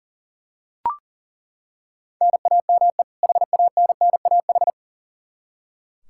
Morse code, heard as a keyed tone, about 40 wpm, 700 Hz. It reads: NAME HANNAH
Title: Practice QSO - Exchange 1 - Beginning - 40wpm